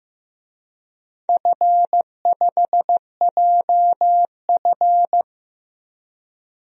Morse code, heard as a keyed tone, about 15 wpm, 700 Hz. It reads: F5JF
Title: DXCC-100 Call Signs - 4 Characters - 15wpm